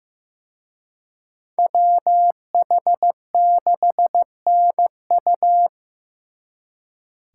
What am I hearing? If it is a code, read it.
Morse code, heard as a keyed tone, about 18 wpm, 700 Hz. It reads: WH6NU
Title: Speed Racing - USA 2x2 Call Signs 15wpm